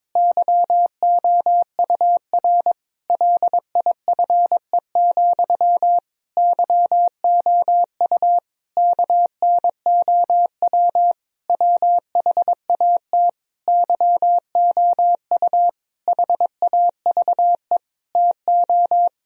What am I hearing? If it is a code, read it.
Morse code, heard as a keyed tone, about 22 wpm, 700 Hz. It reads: YOUR LIFE, YOU KNOW WHAT YOU HAVE TO
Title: Sentences from Top 1000 Words - Review Part 3 - 22wpm